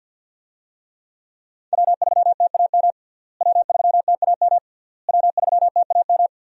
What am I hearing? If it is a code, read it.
Morse code, heard as a keyed tone, about 47 wpm, 700 Hz. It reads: W3TAM W3TAM W3TAM
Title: USA Call Signs - 5 Characters - Repeated 3x - 50wpm